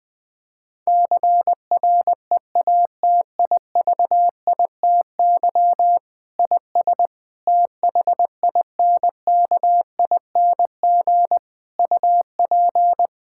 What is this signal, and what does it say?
Morse code, heard as a keyed tone, about 21 wpm, 700 Hz. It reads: CREATIVITY IS THINKING UP
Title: Difficult Sentences with Hints - 20wpm